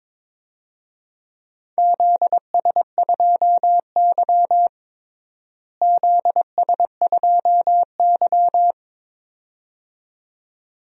Morse code, heard as a keyed tone, about 21 wpm, 700 Hz. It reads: ZS2Y ZS2Y
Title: DXCC-100 Call Signs - 4 Characters - Repeated 2x - 22wpm